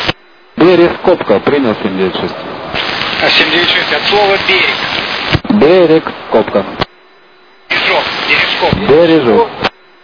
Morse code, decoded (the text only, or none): none